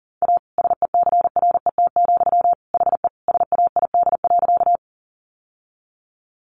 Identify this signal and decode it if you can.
Morse code, heard as a keyed tone, about 40 wpm, 700 Hz. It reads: A SECRET, HE SAID.
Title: Sentences from Top 1200 Words 40wpm